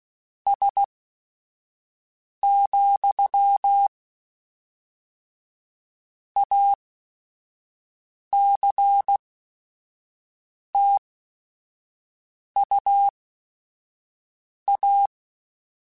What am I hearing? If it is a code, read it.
Morse code, heard as a keyed tone, about 16 wpm, spaced as for 5 wpm, 800 Hz. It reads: S, ACTUA
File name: Technical Subjects / Technical Subjects-mp3s / Sw Pwr Sup @5